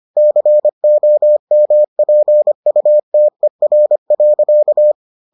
Morse code, heard as a keyed tone, about 25 wpm, 600 Hz. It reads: COMPUTER.